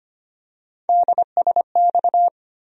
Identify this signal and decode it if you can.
Morse code, heard as a keyed tone, about 25 wpm, 700 Hz. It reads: DSX
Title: USA Call Sign Postfixes - 3 Letters 25wpm